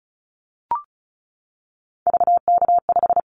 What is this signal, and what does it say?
Morse code, heard as a keyed tone, about 36 wpm, 700 Hz. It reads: VK5
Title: Callsign Prefixes for Top 100 DXCC - 35wpm